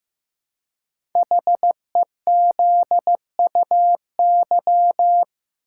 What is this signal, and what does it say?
Morse code, heard as a keyed tone, about 15 wpm, 700 Hz. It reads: HEZUY
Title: Code Groups - 5 Letters - 15wpm